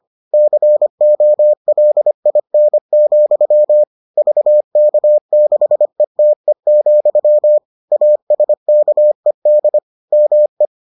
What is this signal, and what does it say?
Morse code, heard as a keyed tone, about 25 wpm, 600 Hz. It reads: COLIN, VK6ETE, ASKED ME